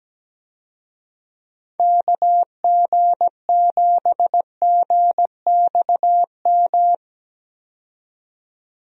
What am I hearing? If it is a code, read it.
Morse code, heard as a keyed tone, about 17 wpm, 700 Hz. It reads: KG7GXM
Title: USA Call Signs - 6 Characters - 17wpm